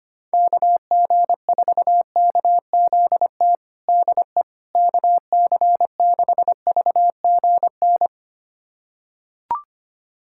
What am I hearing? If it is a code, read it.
Morse code, heard as a keyed tone, about 25 wpm, 700 Hz. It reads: KG4KZT DE KC6VGN
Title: USA Callsign DE Callsign 25wpm